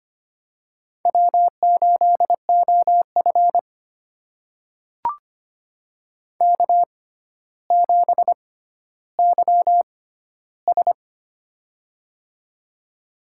Morse code, heard as a keyed tone, about 25 wpm, spaced as for 13 wpm, 700 Hz. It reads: W8OF K7YS
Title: USA Call Signs - 4 Characters - 6x Character Spacing - 25wpm